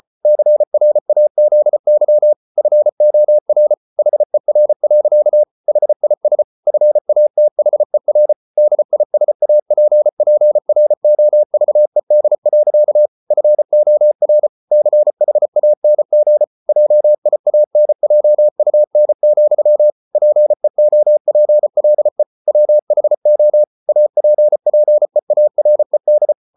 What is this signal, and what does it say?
Morse code, heard as a keyed tone, about 34 wpm, 600 Hz. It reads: CRAZY FOR HER. HIS FATHER DISAPPROVED. FOR CHANG JIANJUN, PEOPLE WHO APPEARED